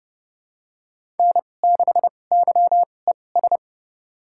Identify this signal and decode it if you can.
Morse code, heard as a keyed tone, about 30 wpm, spaced as for 22 wpm, 700 Hz. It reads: N6YES
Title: USA Call Signs - 5 Characters - 2x Character Spacing - 30wpm